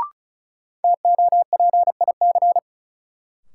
Morse code, heard as a keyed tone, about 36 wpm, 700 Hz. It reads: TOPIC
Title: Sentences from Top 1200 Words - New Word Drill Alt - 35wpm